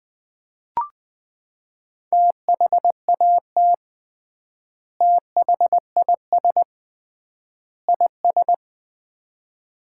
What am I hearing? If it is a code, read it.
Morse code, heard as a keyed tone, about 20 wpm, 700 Hz. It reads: THAT THIS IS
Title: 3x Word-Spacing - Sets of 3 Words - 20wpm